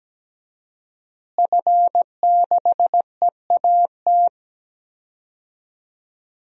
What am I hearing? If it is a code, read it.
Morse code, heard as a keyed tone, about 17 wpm, 700 Hz. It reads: F6EAT
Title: DXCC-100 Call Signs - 5 Characters - Repeated 3x - 17wpm